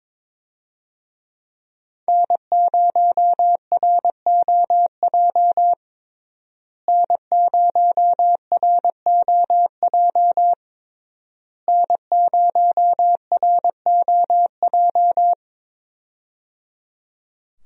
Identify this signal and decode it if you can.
Morse code, heard as a keyed tone, about 22 wpm, 700 Hz. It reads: N0ROJ N0ROJ N0ROJ
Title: USA Call Signs - 5 Characters - Repeated 3x - 22wpm